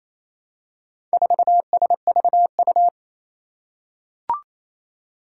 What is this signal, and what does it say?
Morse code, heard as a keyed tone, about 27 wpm, 700 Hz. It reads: 4SVU
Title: Four Challenging Letter-Number Combinations 28wpm